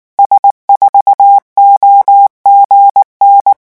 Morse code, heard as a keyed tone, about 19 wpm, 800 Hz. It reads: S4OGN
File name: morse mp3 / revision12-vitesse 18 mots